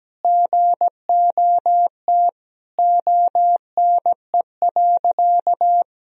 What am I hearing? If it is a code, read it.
Morse code, heard as a keyed tone, about 17 wpm, 700 Hz. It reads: GOT ONE.
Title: Sentences from Top 300 Words and Abbreviations 17wpm